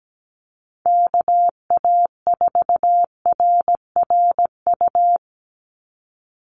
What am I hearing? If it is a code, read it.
Morse code, heard as a keyed tone, about 17 wpm, 700 Hz. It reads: KA4RRU
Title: Call Signs Repeated 3x - 17wpm